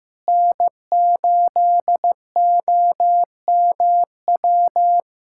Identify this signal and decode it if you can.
Morse code, heard as a keyed tone, about 15 wpm, 700 Hz. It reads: N8OMW
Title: USA Callsign DE Callsign - 5 Characters - 15wpm